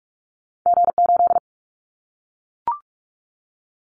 Morse code, heard as a keyed tone, about 44 wpm, 700 Hz. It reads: G8
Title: Callsign Prefixes for Top 25 DXCC - 45wpm